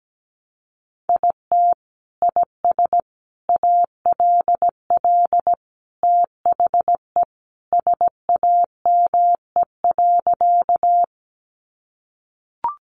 Morse code, heard as a keyed tone, about 17 wpm, 700 Hz. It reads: IT IS ALL THE SAME.
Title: Sentences from Top 200 Words 17wpm